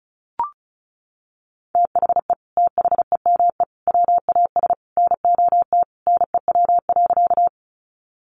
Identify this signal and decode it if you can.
Morse code, heard as a keyed tone, about 34 wpm, 700 Hz. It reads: THE THEME WAS NOT NEW.
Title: Sentences from Top 1100 Words 35wpm